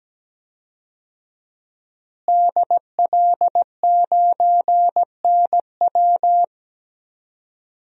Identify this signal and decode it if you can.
Morse code, heard as a keyed tone, about 17 wpm, 700 Hz. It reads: DL9NW
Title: DXCC-100 Call Signs - 5 Characters - Repeated 2x - 17wpm